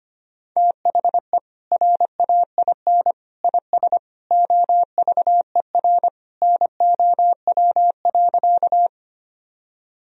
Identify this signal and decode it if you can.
Morse code, heard as a keyed tone, about 25 wpm, 700 Hz. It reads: THE RAIN IS OVER NOW.
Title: Sentences from Top 1000 Words - Review Part 1 - 25wpm